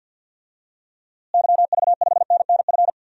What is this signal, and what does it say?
Morse code, heard as a keyed tone, about 51 wpm, 700 Hz. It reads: YV5NNF
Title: DXCC-100 Call Signs - 6 Characters - Repeated 3x - 50wpm